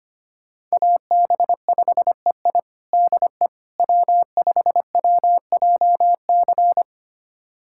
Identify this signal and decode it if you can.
Morse code, heard as a keyed tone, about 25 wpm, 700 Hz. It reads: AB5EI DE W5WJC
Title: USA Callsign DE Callsign - 5 Characters - 25wpm